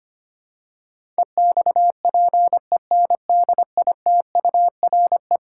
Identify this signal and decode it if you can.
Morse code, heard as a keyed tone, about 25 wpm, 700 Hz. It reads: EXPENDITURE
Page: morse-code-ninja-practice-25wpm